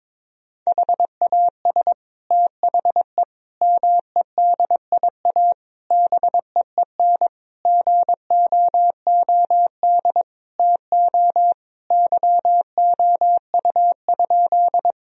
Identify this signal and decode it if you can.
Morse code, heard as a keyed tone, about 22 wpm, 700 Hz. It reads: HAS THE MEDIA BEEN GOOD TO YOU?
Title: Sentences from Top 1000 Words - Review Part 4 - 22wpm